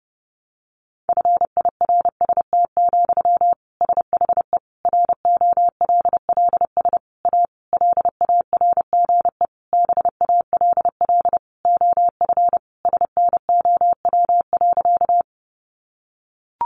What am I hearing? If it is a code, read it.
Morse code, heard as a keyed tone, about 30 wpm, 700 Hz. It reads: FIRST, SHE ROLLS A LARGE BALL OF SNOW.
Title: Easy Phrases 30wpm